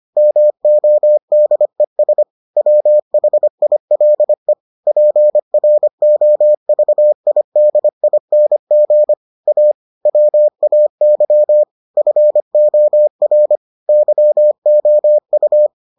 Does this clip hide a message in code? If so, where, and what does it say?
Yes: Morse code from 0.2 to 15.7 s, heard as a keyed tone, about 25 wpm, 600 Hz. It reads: MODES WHILE PROVIDING A WAY FOR YOU